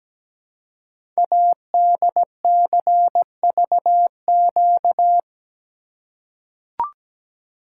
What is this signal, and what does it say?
Morse code, heard as a keyed tone, about 17 wpm, 700 Hz. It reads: ADCVQ
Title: Code Groups - 5 Letters - 17wpm